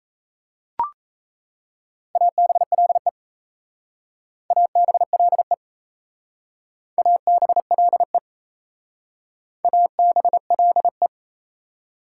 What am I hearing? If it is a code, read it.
Morse code, heard as a keyed tone, about 35 wpm, 700 Hz. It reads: ABLE ABLE ABLE ABLE
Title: Speed-Racing Prefixes and Suffixes 28wpm